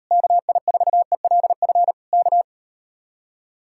Morse code, heard as a keyed tone, about 38 wpm, 700 Hz. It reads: KI4ELF K